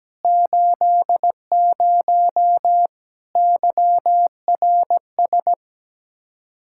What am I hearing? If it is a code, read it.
Morse code, heard as a keyed tone, about 17 wpm, 700 Hz. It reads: 80 YRS
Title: K1USN SST QSO Practice 17wpm